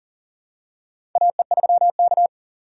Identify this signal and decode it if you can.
Morse code, heard as a keyed tone, about 40 wpm, 690 Hz. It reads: AE3K
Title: USA Call Signs - 4 Characters - 2x Character Spacing - 40wpm